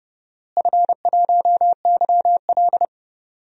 Morse code, heard as a keyed tone, about 30 wpm, 700 Hz. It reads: F1YL
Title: DXCC-100 Call Signs - 4 Characters - Repeated 3x - 30wpm